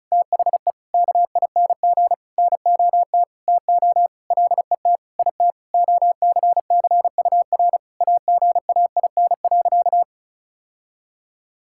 Morse code, heard as a keyed tone, about 36 wpm, 700 Hz. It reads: THE KING NOT TO LET IT OCCUR AGAIN.
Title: Sentences from Top 1000 Words - Review Part 6 - 35wpm